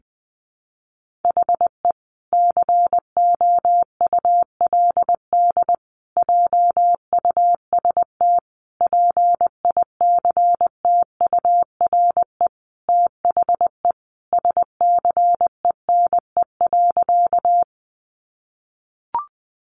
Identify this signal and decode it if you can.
Morse code, heard as a keyed tone, about 20 wpm, 700 Hz. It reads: HE COULD JUST PICTURE THE SCENE.
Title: Sentences from Top 1100 Words 20wpm